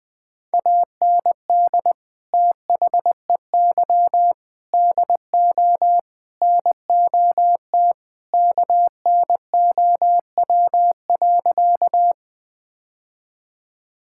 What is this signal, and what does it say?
Morse code, heard as a keyed tone, about 20 wpm, 700 Hz. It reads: AND THEY DO NOT KNOW.
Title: Sentences with 4-Word Phrases - 20wpm